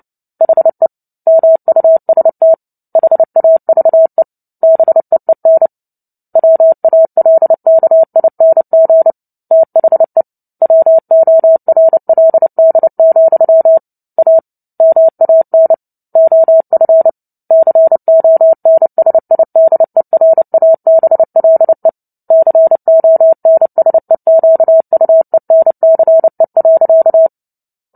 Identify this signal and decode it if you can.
Morse code, heard as a keyed tone, about 30 wpm, 650 Hz. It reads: HE MUST HAVE BEEN WALKING THE WORLD, A MAN OF CONSIDERABLE CONSEQUENCE.
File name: Book Two / 03-Chapter 3 - 29 WPM